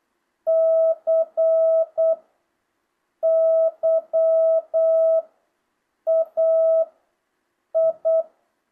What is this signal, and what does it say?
Morse code, heard as a keyed tone, about 8 wpm, 660 Hz. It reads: C Y A I
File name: jh8cbh / 2025cw